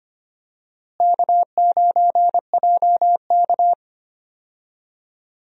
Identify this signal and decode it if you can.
Morse code, heard as a keyed tone, about 25 wpm, 700 Hz. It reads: K9JK
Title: USA Call Signs - 4 Characters - Repeated 2x - 25wpm